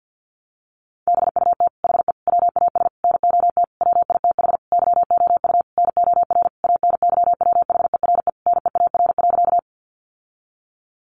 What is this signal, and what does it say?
Morse code, heard as a keyed tone, about 50 wpm, 700 Hz. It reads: BUT HE WAS NOT WITH YOU NOR ANYWHERE NEAR.
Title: Sentences from Top 1500 Words 50wpm